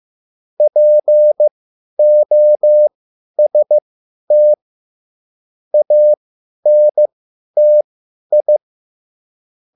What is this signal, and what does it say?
Morse code, heard as a keyed tone, about 15 wpm, spaced as for 10 wpm, 600 Hz. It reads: POST ANTI